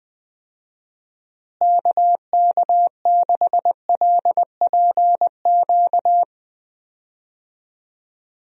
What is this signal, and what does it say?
Morse code, heard as a keyed tone, about 20 wpm, 700 Hz. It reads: KK6LPQ
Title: USA Call Signs - 6 Characters - Repeated 2x - 20wpm